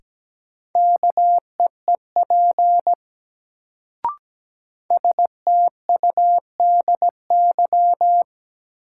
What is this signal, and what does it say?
Morse code, heard as a keyed tone, about 17 wpm, 700 Hz. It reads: KEEP STUDY
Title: Sentences from Top 300 Words - New Word Drill - 17wpm